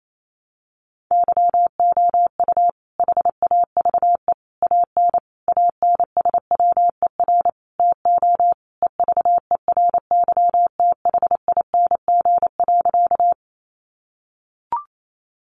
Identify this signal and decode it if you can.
Morse code, heard as a keyed tone, about 28 wpm, 700 Hz. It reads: YOU HAVE AN ANSWER TO EVERYTHING.
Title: Sentences from Top 700 Words 28wpm